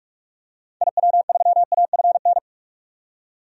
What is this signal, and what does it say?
Morse code, heard as a keyed tone, about 41 wpm, 700 Hz. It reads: IW3AFN
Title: DXCC-100 Call Signs - 6 Characters - 45wpm